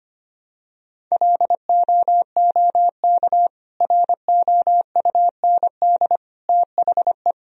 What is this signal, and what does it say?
Morse code, heard as a keyed tone, about 25 wpm, 700 Hz. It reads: LOOK ROUND THE
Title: Sentences from Top 1000 Words - Review Part 1 - 25wpm